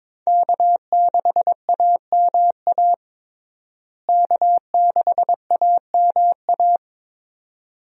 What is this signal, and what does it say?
Morse code, heard as a keyed tone, about 21 wpm, 700 Hz. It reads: K6AMA K6AMA
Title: USA Call Signs - 5 Characters - Repeated 2x - 22wpm